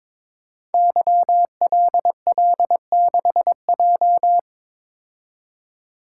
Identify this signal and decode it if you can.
Morse code, heard as a keyed tone, about 22 wpm, 700 Hz. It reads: YLL6J
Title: Code Groups - 5 Letter:Numbers - 22wpm